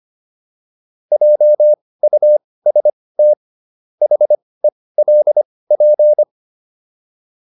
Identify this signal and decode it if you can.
Morse code, heard as a keyed tone, about 25 wpm, spaced as for 18 wpm, 600 Hz. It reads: JUST HELP
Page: intermediate-practice-files